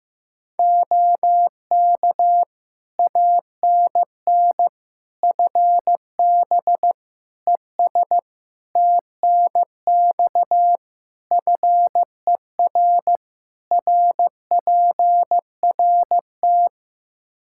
Morse code, heard as a keyed tone, about 15 wpm, 700 Hz. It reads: OK ANN FB ES TNX FER RPRT
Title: K1USN SST QSO Practice 15wpm